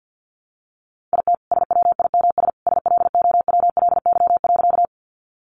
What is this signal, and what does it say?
Morse code, heard as a keyed tone, about 50 wpm, 700 Hz. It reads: IT SWIMS SLOWLY.
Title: Additional Easy Phrases 50wpm